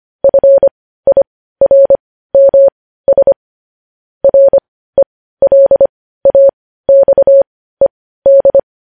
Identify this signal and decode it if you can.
Morse code, heard as a keyed tone, about 25 wpm, spaced as for 15 wpm, 550 Hz. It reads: FIRMS RELAXED
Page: news-headlines-in-morse-code-at-15-wpm